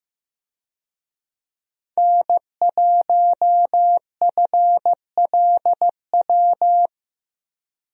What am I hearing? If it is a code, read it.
Morse code, heard as a keyed tone, about 16 wpm, 700 Hz. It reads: N1FLW
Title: USA Call Signs - 5 Characters - Repeated 3x - 15wpm